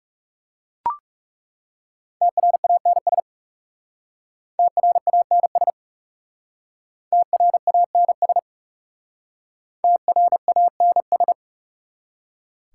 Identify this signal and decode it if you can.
Morse code, heard as a keyed tone, about 37 wpm, 700 Hz. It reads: TRANS TRANS TRANS TRANS
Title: Speed-Racing Prefixes and Suffixes 30wpm